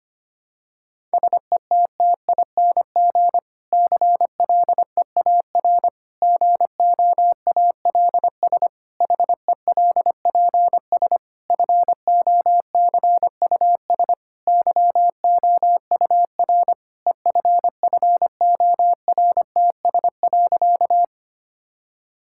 Morse code, heard as a keyed tone, about 25 wpm, 700 Hz. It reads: SETTING CLEAR GOALS HELPS FOCUS YOUR EFFORTS.
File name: practice / 25wpm / Difficult Sentences with Hints - 25wpm